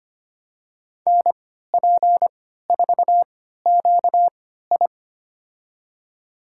Morse code, heard as a keyed tone, about 25 wpm, spaced as for 14 wpm, 700 Hz. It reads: NP4QI
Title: USA Call Signs - 5 Characters - 3x Character Spacing - 25wpm